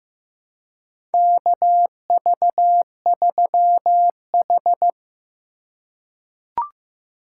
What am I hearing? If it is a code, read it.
Morse code, heard as a keyed tone, about 15 wpm, 700 Hz. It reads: KV3H
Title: USA Call Signs - 4 Characters - 3x Character Spacing - 15wpm